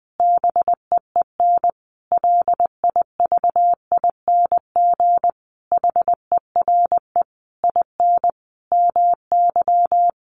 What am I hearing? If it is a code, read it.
Morse code, heard as a keyed tone, about 20 wpm, 700 Hz. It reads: BEEN LIVING HERE IN MY
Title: Sentences from Top 1100 Words 20wpm